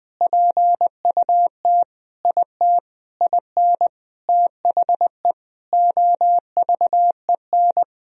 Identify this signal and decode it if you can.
Morse code, heard as a keyed tone, about 20 wpm, 700 Hz. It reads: PUT IT IN THE OVEN
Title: Sentences from Top 1000 Words - Review Part 4 - 20wpm